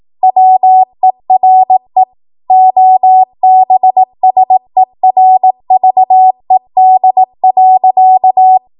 Morse code, heard as a keyed tone, about 18 wpm, 750 Hz. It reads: WERE OBSERVED.
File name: Morse / 18WPM